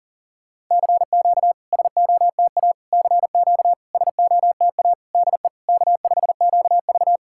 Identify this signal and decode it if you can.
Morse code, heard as a keyed tone, about 40 wpm, 700 Hz. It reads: CQ SOTA CQ SOTA DE K5QV
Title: SOTA QSO Practice 40wpm